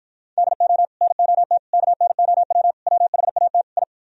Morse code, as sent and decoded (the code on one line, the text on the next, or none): -.. --- / -. --- - / -.- -. --- .-- / .-- .... .- - / ..
DO NOT KNOW WHAT I